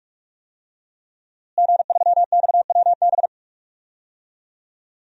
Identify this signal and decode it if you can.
Morse code, heard as a keyed tone, about 41 wpm, 700 Hz. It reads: G3XWB
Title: DXCC-100 Call Signs - 5 Characters - Repeated 3x - 45wpm